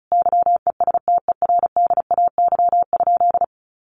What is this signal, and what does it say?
Morse code, heard as a keyed tone, about 34 wpm, 700 Hz. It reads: YESTERDAY?
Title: A Sound Of Thunder 35wpm